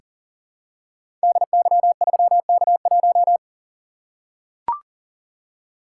Morse code, heard as a keyed tone, about 41 wpm, 700 Hz. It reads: DY3K1
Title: Code Groups - 5 Letter:Numbers - 40wpm